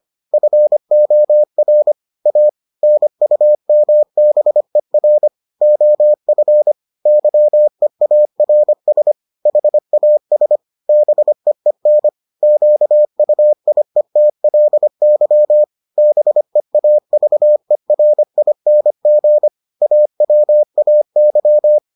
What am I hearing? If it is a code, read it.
Morse code, heard as a keyed tone, about 25 wpm, 600 Hz. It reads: FOR A NUMBER OF YEARS HAS BEEN QUIETLY BEAVERING AWAY